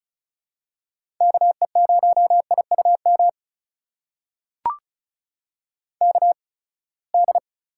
Morse code, heard as a keyed tone, about 36 wpm, spaced as for 24 wpm, 700 Hz. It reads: KE0IUM KD
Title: USA Call Signs - 6 Characters - 8x Character Spacing - 35wpm